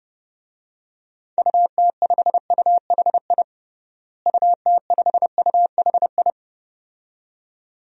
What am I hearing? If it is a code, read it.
Morse code, heard as a keyed tone, about 31 wpm, 700 Hz. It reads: UT5UHI UT5UHI
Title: DXCC-100 Call Signs - 6 Characters - Repeated 2x - 30wpm